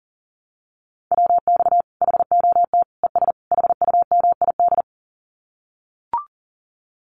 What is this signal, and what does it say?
Morse code, heard as a keyed tone, about 40 wpm, 700 Hz. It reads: WX HOT ES HUMID
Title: Common QSO Phrases 40wpm